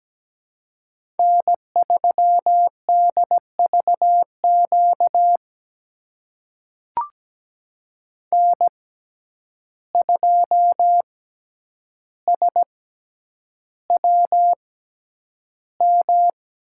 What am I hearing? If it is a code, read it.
Morse code, heard as a keyed tone, about 17 wpm, spaced as for 9 wpm, 700 Hz. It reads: N3DVQ N2SWM
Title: USA Call Signs - 5 Characters - 6x Character Spacing - 17wpm